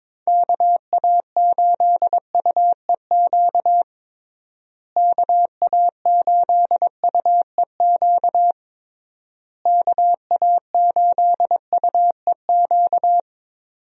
Morse code, heard as a keyed tone, about 21 wpm, 700 Hz. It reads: KA8UEQ KA8UEQ KA8UEQ
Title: USA Call Signs - 6 Characters - Repeated 3x - 22wpm